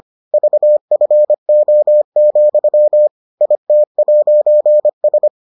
Morse code, heard as a keyed tone, about 25 wpm, 600 Hz. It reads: VFO, IT'S